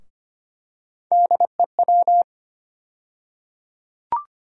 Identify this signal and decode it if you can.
Morse code, heard as a keyed tone, about 25 wpm, 700 Hz. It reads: DEW
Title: USA Call Sign Postfixes - 3 Letters 25wpm